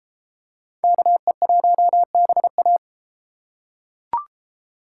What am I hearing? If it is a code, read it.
Morse code, heard as a keyed tone, about 33 wpm, 700 Hz. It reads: KE1BA